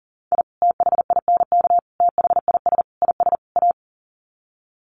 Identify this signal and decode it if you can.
Morse code, heard as a keyed tone, about 40 wpm, 700 Hz. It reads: I THINK THIS IS A
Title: Sets of 5 Words 40wpm